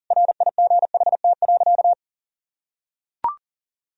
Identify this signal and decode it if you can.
Morse code, heard as a keyed tone, about 40 wpm, 700 Hz. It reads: RIGHT.
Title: Sentences from Top 1000 Words - Review Part 4 - 40wpm